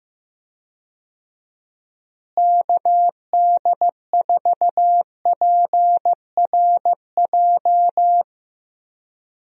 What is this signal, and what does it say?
Morse code, heard as a keyed tone, about 16 wpm, 700 Hz. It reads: KD4PRJ
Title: USA Call Signs - 6 Characters - Repeated 2x - 15wpm